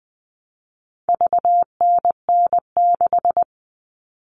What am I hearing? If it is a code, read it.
Morse code, heard as a keyed tone, about 20 wpm, 700 Hz. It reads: VNN6
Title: All Cut Numbers - 4-Digit 20wpm